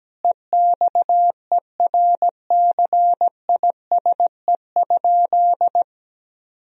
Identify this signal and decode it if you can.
Morse code, heard as a keyed tone, about 17 wpm, 700 Hz. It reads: EXERCISE?
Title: Sentences from Top 1000 Words - Review Part 4 - 17wpm